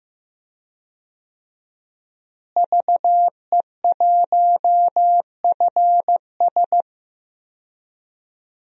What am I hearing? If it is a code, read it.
Morse code, heard as a keyed tone, about 15 wpm, 700 Hz. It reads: VE1FS
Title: DXCC-100 Call Signs - 5 Characters - Repeated 2x - 15wpm